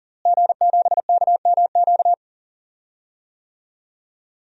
Morse code, heard as a keyed tone, about 40 wpm, 700 Hz. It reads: G7KMQ
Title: DXCC-100 Call Signs - 5 Characters - Repeated 2x - 40wpm